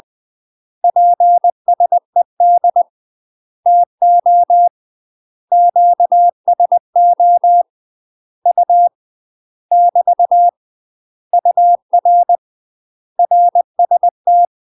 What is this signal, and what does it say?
Morse code, heard as a keyed tone, about 20 wpm, 700 Hz. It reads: PSED TO QSO U <BT> UR RST
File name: UK MASTER QSOs TRAINING / 20WPM / UK MASTER QSOs 0011